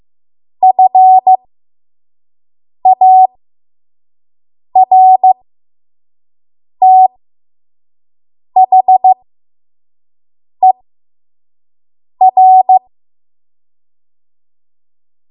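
Morse code, heard as a keyed tone, about 15 wpm, spaced as for 5 wpm, 750 Hz. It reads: FARTHER
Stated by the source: 5 WPM Morse code practice.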